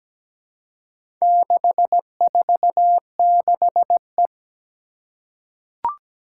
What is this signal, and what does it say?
Morse code, heard as a keyed tone, about 17 wpm, 700 Hz. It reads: 646E
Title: Four Challenging Letter-Number Combinations 17wpm